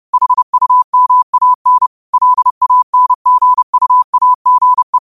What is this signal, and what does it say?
Morse code, heard as a keyed tone, about 30 wpm, 1000 Hz. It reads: HUMAN LANGUAGE
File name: qotdfiles / 30wpm